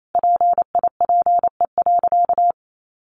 Morse code, heard as a keyed tone, about 28 wpm, 700 Hz. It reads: PIPE.
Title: Sentences from Top 1500 Words 28wpm